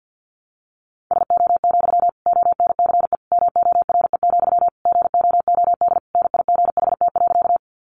Morse code, heard as a keyed tone, about 50 wpm, 700 Hz. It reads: SO, ONCE MORE, GOOD NIGHT.
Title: Sentences from Top 300 Words 50wpm